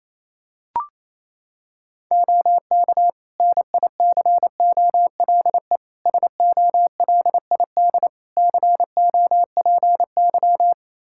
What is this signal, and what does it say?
Morse code, heard as a keyed tone, about 28 wpm, 700 Hz. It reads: OK NICOLE SOLID COPY
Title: Practice QSO - Exchange 3 - More Info - 28wpm